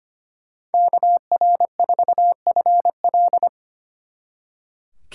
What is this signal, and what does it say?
Morse code, heard as a keyed tone, about 30 wpm, 700 Hz. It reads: KR4FL
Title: Speed Racing - USA 2x2 Call Signs 25wpm